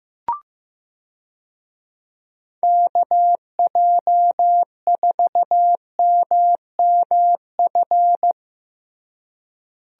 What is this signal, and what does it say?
Morse code, heard as a keyed tone, about 15 wpm, 700 Hz. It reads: KJ4MMF
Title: USA Call Signs - 6 Characters - Repeated 3x - 15wpm